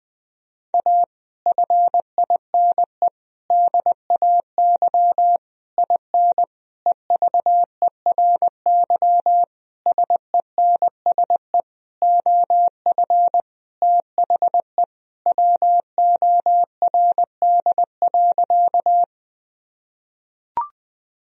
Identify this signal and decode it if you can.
Morse code, heard as a keyed tone, about 20 wpm, 700 Hz. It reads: A FINE DAY IN EVERY SENSE OF THE WORD.
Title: Sentences from Top 1000 Words - Review Part 4 - 20wpm